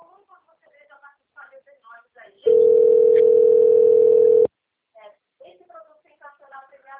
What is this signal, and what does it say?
Ringback tone (North America), 440 + 480 Hz, with speech.
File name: out- - - - - .39096